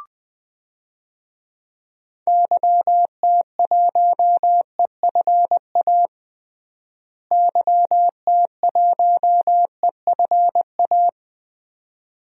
Morse code, heard as a keyed tone, about 20 wpm, 700 Hz. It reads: YT1EFA YT1EFA
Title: DXCC-100 Call Signs - 6 Characters - Repeated 2x - 20wpm